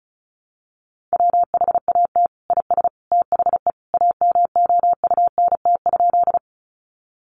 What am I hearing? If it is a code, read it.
Morse code, heard as a keyed tone, about 34 wpm, 700 Hz. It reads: WHAT IS THE AMOUNT?
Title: Sentences from Top 700 Words 35wpm